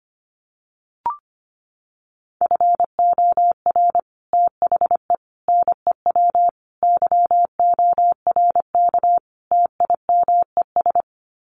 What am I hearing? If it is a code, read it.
Morse code, heard as a keyed tone, about 25 wpm, 700 Hz. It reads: FOR THE NEW YORK TIMES
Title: Sets of 5 Words 25wpm